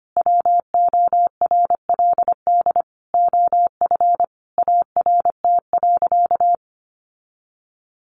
Morse code, heard as a keyed tone, about 25 wpm, 700 Hz. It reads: WORLD OF ART.
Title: Sentences from Top 1500 Words 25wpm